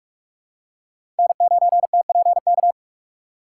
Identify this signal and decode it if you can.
Morse code, heard as a keyed tone, about 45 wpm, 700 Hz. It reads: N9TPK
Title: USA Call Signs - 5 Characters - 4x Character Spacing - 45wpm